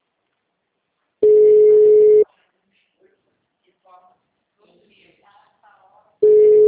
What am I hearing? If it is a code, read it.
Ringback tone (Europe, 425 Hz), with speech.